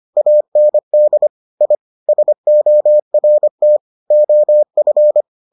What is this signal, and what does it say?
Morse code, heard as a keyed tone, about 25 wpm, 600 Hz. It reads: AND I SORT OF